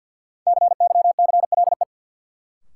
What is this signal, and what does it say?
Morse code, heard as a keyed tone, about 53 wpm, 700 Hz. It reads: CYCLE
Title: Sentences from Top 900 Words - New Word Drill - 50wpm